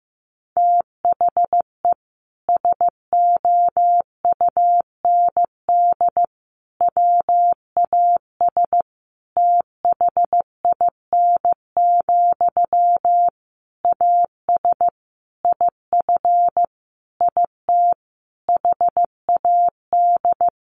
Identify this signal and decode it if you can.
Morse code, heard as a keyed tone, about 15 wpm, 700 Hz. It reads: THE SOUND WAS THIN, AS IF IT HAD
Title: Sentences from Top 1200 Words 15wpm